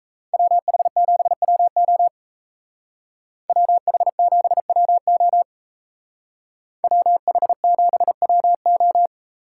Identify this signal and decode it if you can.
Morse code, heard as a keyed tone, about 36 wpm, 700 Hz. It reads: WH7WO WH7WO WH7WO